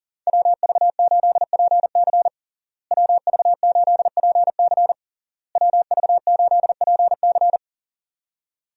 Morse code, heard as a keyed tone, about 37 wpm, 700 Hz. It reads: WV8PC WV8PC WV8PC
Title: USA Call Signs - 5 Characters - Repeated 3x - 40wpm